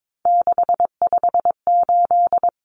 Morse code, heard as a keyed tone, about 22 wpm, 700 Hz. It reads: 658
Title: Numbers - 3-Digit 22wpm